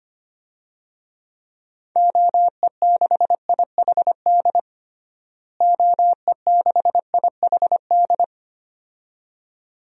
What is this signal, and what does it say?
Morse code, heard as a keyed tone, about 25 wpm, 700 Hz. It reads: OE6IHD OE6IHD
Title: DXCC-100 Call Signs - 6 Characters - Repeated 2x - 25wpm